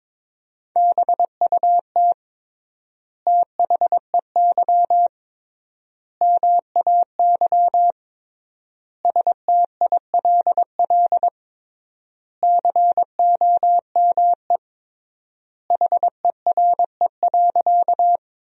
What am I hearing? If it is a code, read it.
Morse code, heard as a keyed tone, about 22 wpm, 700 Hz. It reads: BUT THEY MAY STILL COME HERE.
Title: 3x Word-Spacing - Sentences from Top 300 Words - 22wpm